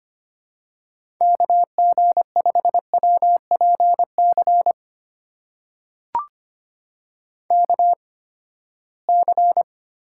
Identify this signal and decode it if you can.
Morse code, heard as a keyed tone, about 25 wpm, spaced as for 17 wpm, 700 Hz. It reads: KG5WPC KC
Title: USA Call Signs - 6 Characters - 8x Character Spacing - 25wpm